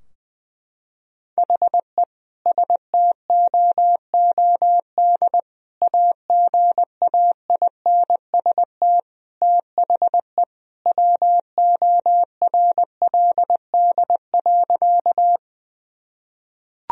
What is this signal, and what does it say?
Morse code, heard as a keyed tone, about 21 wpm, 700 Hz. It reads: HE STOOD AGAINST THE WORLD.
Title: Sentences from Top 1000 Words - Review Part 2 - 20wpm